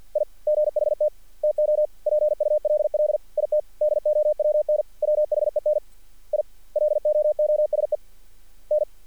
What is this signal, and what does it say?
Morse code, heard as a keyed tone, about 48 wpm, 590 Hz. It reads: I GHT TO PULL IT DOWN WHEN I LOOSE N
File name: archived / CRSnet